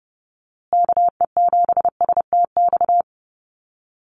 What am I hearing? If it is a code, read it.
Morse code, heard as a keyed tone, about 30 wpm, 700 Hz. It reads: KE7STX
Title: USA Call Signs Repeated 2x 30wpm